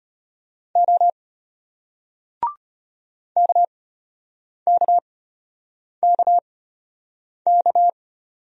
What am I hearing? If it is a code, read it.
Morse code, heard as a keyed tone, about 34 wpm, spaced as for 8 wpm, 700 Hz. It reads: O KKKK